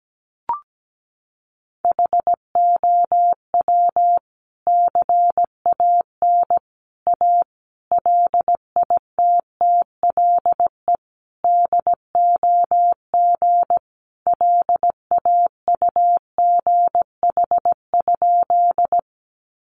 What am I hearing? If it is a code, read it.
Morse code, heard as a keyed tone, about 17 wpm, 700 Hz. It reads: HOW CAN A LITTLE DOG LAUGH?
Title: Sentences from Top 500 Words 17wpm